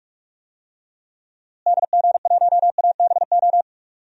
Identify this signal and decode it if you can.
Morse code, heard as a keyed tone, about 45 wpm, 700 Hz. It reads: DG1ABO
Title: DXCC-100 Call Signs - 6 Characters - Repeated 3x - 45wpm